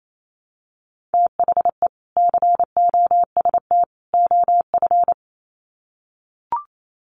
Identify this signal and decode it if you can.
Morse code, heard as a keyed tone, about 28 wpm, 700 Hz. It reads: THE COST OF
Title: Sets of 3 Words 28wpm